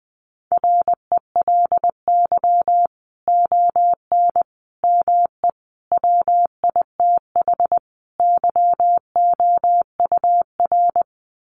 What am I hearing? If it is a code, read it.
Morse code, heard as a keyed tone, about 20 wpm, 700 Hz. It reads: RELY ON ME WITH YOUR
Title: Sentences from Top 1200 Words 20wpm